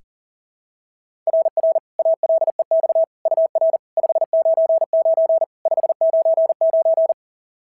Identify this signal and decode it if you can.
Morse code, heard as a keyed tone, about 34 wpm, 660 Hz. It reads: RR ALEX UR 599 599
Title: SOTA QSO Practice 40wpm